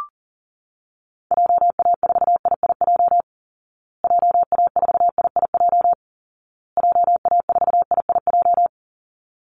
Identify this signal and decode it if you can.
Morse code, heard as a keyed tone, about 40 wpm, 700 Hz. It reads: JA4IIJ JA4IIJ JA4IIJ
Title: Call Signs Repeated 2x - 40wpm